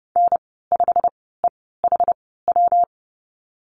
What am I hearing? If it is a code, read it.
Morse code, heard as a keyed tone, about 30 wpm, spaced as for 17 wpm, 700 Hz. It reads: N5EHW
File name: Lesson 19g - Letter w - Call Signs 30wpm